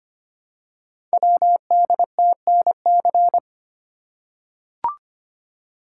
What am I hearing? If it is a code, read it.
Morse code, heard as a keyed tone, about 25 wpm, 700 Hz. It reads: WDTNC
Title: Code Groups - 5 Letters - 25wpm